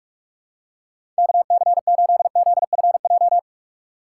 Morse code, heard as a keyed tone, about 46 wpm, 700 Hz. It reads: KC8ZFJ
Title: USA Call Signs - 6 Characters - 2x Character Spacing - 45wpm